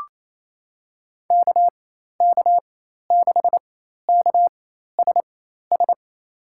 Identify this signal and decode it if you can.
Morse code, heard as a keyed tone, about 28 wpm, spaced as for 13 wpm, 700 Hz. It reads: KK6KSS
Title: USA Call Signs - 4x Character Spacing - 28wpm